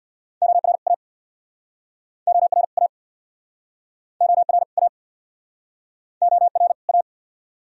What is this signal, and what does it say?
Morse code, heard as a keyed tone, about 69 wpm, 700 Hz. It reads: OR A OR A OR A OR A